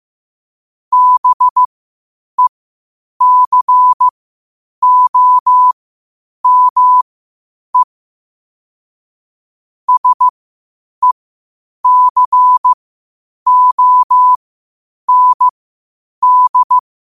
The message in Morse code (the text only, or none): BECOME SECOND